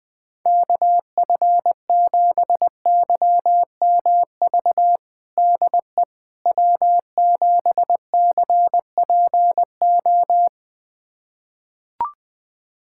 Morse code, heard as a keyed tone, about 20 wpm, 700 Hz. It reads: KF7YMV DE W7CPO
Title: USA Callsign DE Callsign 20wpm